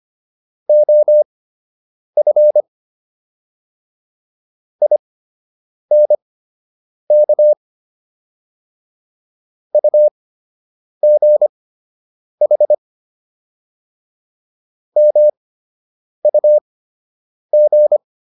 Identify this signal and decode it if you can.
Morse code, heard as a keyed tone, about 25 wpm, spaced as for 8 wpm, 600 Hz. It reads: OF INK UGH MUG